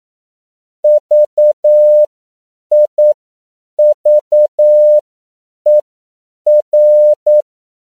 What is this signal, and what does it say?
Morse code, heard as a keyed tone, about 9 wpm, spaced as for 7 wpm, 600 Hz. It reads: VIVER